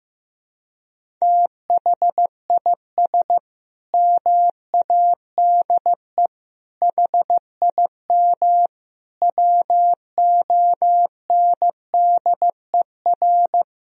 Morse code, heard as a keyed tone, about 15 wpm, 700 Hz. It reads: THIS MADE HIM WONDER
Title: Sentences from Top 500 Words and Abbreviations 15wpm